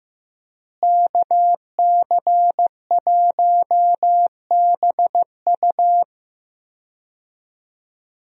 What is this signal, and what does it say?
Morse code, heard as a keyed tone, about 15 wpm, 700 Hz. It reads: KC1BU
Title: USA Call Signs - 5 Characters - Repeated 3x - 15wpm